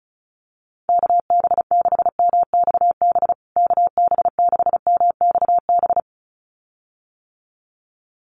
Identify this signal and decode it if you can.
Morse code, heard as a keyed tone, about 34 wpm, 700 Hz. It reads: KB6MXB KB6MXB
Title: USA Call Signs Repeated 2x 35wpm